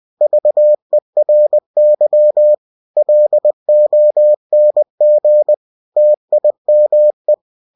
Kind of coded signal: Morse code